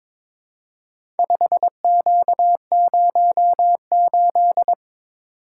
Morse code, heard as a keyed tone, about 23 wpm, 700 Hz. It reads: 5Q08
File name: practice / 22wpm / Four Letter-Number Combinations - 22wpm